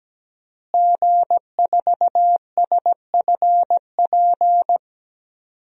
Morse code, heard as a keyed tone, about 17 wpm, 700 Hz. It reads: G4SFP
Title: DXCC-100 Call Signs - 5 Characters - Repeated 2x - 17wpm